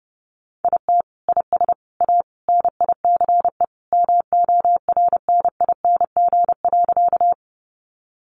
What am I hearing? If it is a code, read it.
Morse code, heard as a keyed tone, about 28 wpm, 700 Hz. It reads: IT IS A NICE MORNING.
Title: Sentences from Top 1100 Words 30wpm